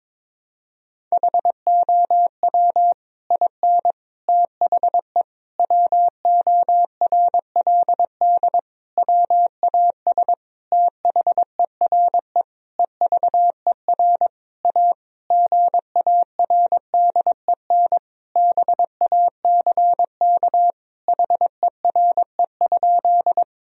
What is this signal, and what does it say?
Morse code, heard as a keyed tone, about 22 wpm, 700 Hz. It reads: HOW IN THE WORLD WAS THERE EVER A GARDEN BACK HERE?
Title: Sentences from Top 1000 Words - Review Part 4 - 22wpm